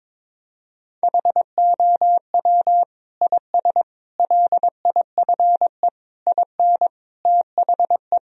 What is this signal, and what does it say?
Morse code, heard as a keyed tone, about 22 wpm, 700 Hz. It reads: HOW IS LIFE IN THE
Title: Sentences from Top 1000 Words - Review Part 2 - 22wpm